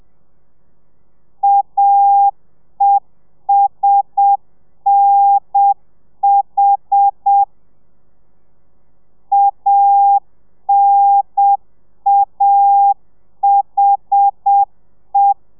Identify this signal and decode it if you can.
Morse code, heard as a keyed tone, about 7 wpm, 800 Hz. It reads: AESNH ANAHE